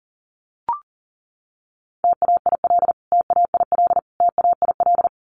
Morse code, heard as a keyed tone, about 40 wpm, 700 Hz. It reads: TAIL TAIL TAIL
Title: Top 500 Words Repeated 40wpm